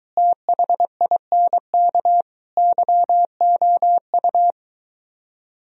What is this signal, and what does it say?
Morse code, heard as a keyed tone, about 22 wpm, 700 Hz. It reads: THINK YOU